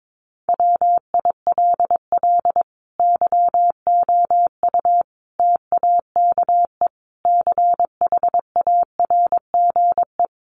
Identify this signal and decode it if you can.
Morse code, heard as a keyed tone, about 22 wpm, 700 Hz. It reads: WILL YOU TAKE CHARGE
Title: Sentences from Top 900 Words 22wpm